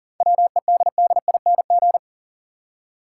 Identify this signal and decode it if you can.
Morse code, heard as a keyed tone, about 40 wpm, 700 Hz. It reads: WEDDING
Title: Seven Letter Words - 40wpm